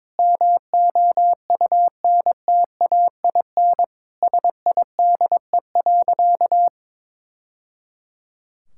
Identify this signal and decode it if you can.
Morse code, heard as a keyed tone, about 22 wpm, 700 Hz. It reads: MOUNTAIN SIDE.
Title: Sentences from Top 1000 Words - Review Part 6 - 22wpm